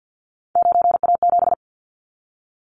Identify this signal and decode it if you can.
Morse code, heard as a keyed tone, about 49 wpm, 700 Hz. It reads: 9A7
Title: Callsign Prefixes for Top 25 DXCC - 50wpm